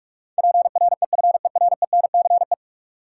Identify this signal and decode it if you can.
Morse code, heard as a keyed tone, about 46 wpm, 700 Hz. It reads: PREFERENCE
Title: Sentences from Top 1500 Words - New Word Drill Alt - 45wpm